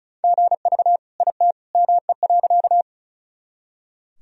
Morse code, heard as a keyed tone, about 36 wpm, 700 Hz. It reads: GV IT ME.
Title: Sentences from Top 200 Words and Abbreviations 35wpm